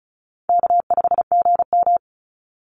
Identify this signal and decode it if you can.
Morse code, heard as a keyed tone, about 34 wpm, 700 Hz. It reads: K5GM
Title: Easy Call Signs Repeated 3x - 35wpm